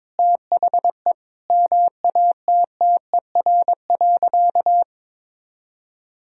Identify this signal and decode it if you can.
Morse code, heard as a keyed tone, about 22 wpm, 700 Hz. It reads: THE MATTER.
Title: Sentences from Top 1000 Words - Review Part 1 - 22wpm